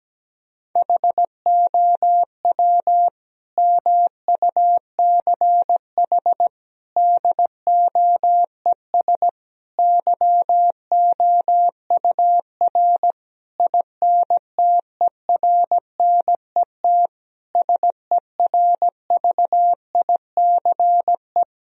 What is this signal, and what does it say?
Morse code, heard as a keyed tone, about 17 wpm, 700 Hz. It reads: HOW MUCH DOES YOUR INTERNET SERVICE
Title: Sentences from Top 1000 Words - Review Part 4 - 17wpm